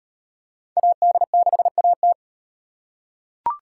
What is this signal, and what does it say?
Morse code, heard as a keyed tone, about 38 wpm, 700 Hz. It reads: AD6AT